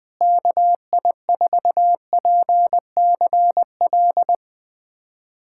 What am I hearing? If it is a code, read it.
Morse code, heard as a keyed tone, about 20 wpm, 700 Hz. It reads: KI4PCL
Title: USA Callsign DE Callsign 20wpm